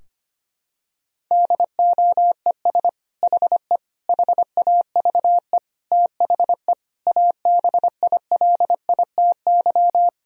Morse code, heard as a keyed tone, about 25 wpm, 700 Hz. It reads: DOES HE HAVE THE ABILITY
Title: Sentences from Top 1000 Words - Review Part 3 - 25wpm